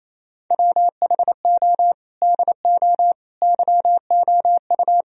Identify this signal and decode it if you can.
Morse code, heard as a keyed tone, about 28 wpm, 700 Hz. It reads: WHO DO YOU
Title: Sentences from Top 1000 Words - Review Part 4 - 28wpm